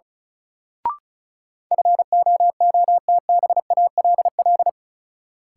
Morse code, heard as a keyed tone, about 36 wpm, 700 Hz. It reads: FOOTBALL
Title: Sentences from Top 1600 Words - New Word Drill Alt - 35wpm